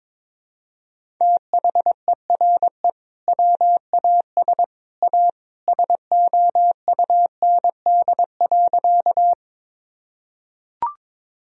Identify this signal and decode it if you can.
Morse code, heard as a keyed tone, about 22 wpm, 700 Hz. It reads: THERE WAS A SOUND.
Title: Sentences from Top 1000 Words - Review Part 1 - 22wpm